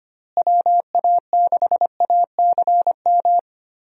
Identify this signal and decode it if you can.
Morse code, heard as a keyed tone, about 25 wpm, 700 Hz. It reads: WA6ACM
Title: USA Call Signs - 6 Characters - Repeated 3x - 25wpm